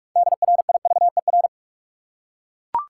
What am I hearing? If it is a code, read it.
Morse code, heard as a keyed tone, about 42 wpm, 700 Hz. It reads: DRIVER
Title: Six Letter Words - 45wpm